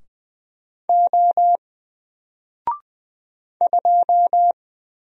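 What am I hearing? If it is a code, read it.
Morse code, heard as a keyed tone, about 20 wpm, 700 Hz. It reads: O 2
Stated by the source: Single letter and number Morse drill at 20 wpm, from N4LCD.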